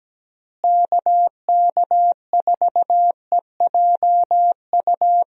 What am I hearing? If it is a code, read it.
Morse code, heard as a keyed tone, about 17 wpm, 700 Hz. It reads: KK4EJU
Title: USA Call Signs - 6 Characters - Repeated 3x - 17wpm